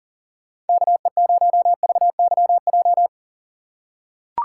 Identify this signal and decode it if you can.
Morse code, heard as a keyed tone, about 38 wpm, 700 Hz. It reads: KE0VYJ
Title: USA Call Signs - 6 Characters - 6x Character Spacing - 40wpm